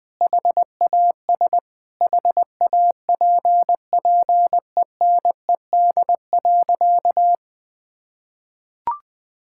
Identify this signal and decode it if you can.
Morse code, heard as a keyed tone, about 20 wpm, 700 Hz. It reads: HAS HAPPENED.
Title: Sentences with 8-Word Phrases - 20wpm